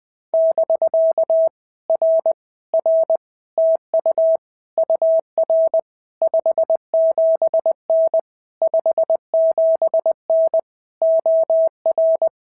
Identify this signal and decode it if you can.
Morse code, heard as a keyed tone, about 19 wpm, 650 Hz. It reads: <BK> R R TU UR 57N 57N OR
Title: POTA QSO Practice - 20wpm